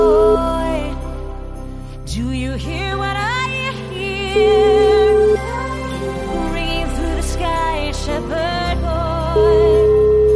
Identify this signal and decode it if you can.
Ringback tone (Europe, 425 Hz), with noise.